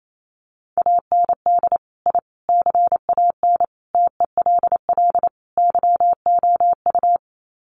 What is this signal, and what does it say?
Morse code, heard as a keyed tone, about 28 wpm, 700 Hz. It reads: AND I CAN TELL YOU
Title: Sets of 5 Words Encore2 28wpm